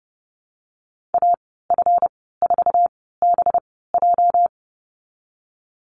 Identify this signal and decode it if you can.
Morse code, heard as a keyed tone, about 30 wpm, spaced as for 17 wpm, 700 Hz. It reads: AF4BJ
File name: Lesson 37g - Letter x - Call Signs 30wpm